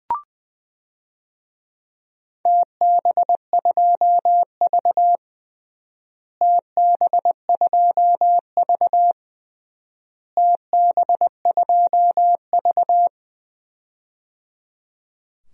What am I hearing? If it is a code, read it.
Morse code, heard as a keyed tone, about 20 wpm, 700 Hz. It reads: TB2V TB2V TB2V
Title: DXCC-100 Call Signs - 4 Characters - Repeated 3x - 20wpm